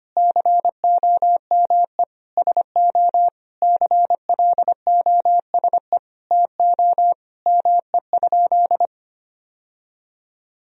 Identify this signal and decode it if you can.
Morse code, heard as a keyed tone, about 25 wpm, 700 Hz. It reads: COME SO CLOSE TO ME?
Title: Sentences from Top 1000 Words - Review Part 1 - 25wpm